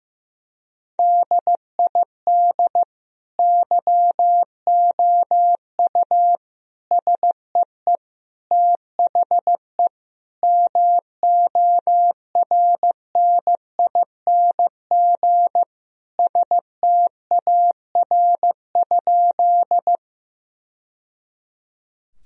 Morse code, heard as a keyed tone, about 15 wpm, 700 Hz. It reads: DID YOU SEE THE MORNING STAR?
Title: Sentences from Top 1000 Words - Review Part 2 - 15wpm